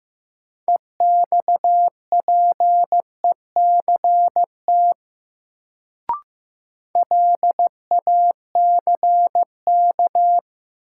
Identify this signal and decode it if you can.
Morse code, heard as a keyed tone, about 15 wpm, 700 Hz. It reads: EXPECT LACK
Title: Sentences from Top 1100 Words - New Word Drill - 15wpm